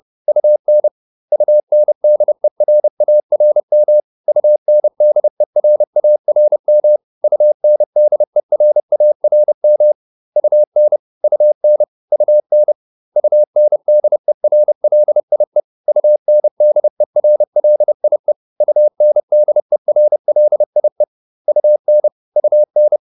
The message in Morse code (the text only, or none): UN UNDERARM UNDERARM UNDERARM UN UN UN UNDERLIE UNDERLIE UNDERLIE UN UN